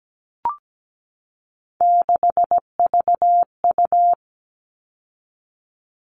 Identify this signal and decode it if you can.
Morse code, heard as a keyed tone, about 17 wpm, 700 Hz. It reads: 6VU
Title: All Cut Numbers - 3-Digit 17wpm